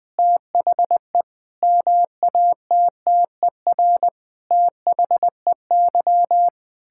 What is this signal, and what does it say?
Morse code, heard as a keyed tone, about 19 wpm, 700 Hz. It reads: THE MATTER THEY
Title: Sentences with 7-Word Phrases - Spoken Beginning - 20wpm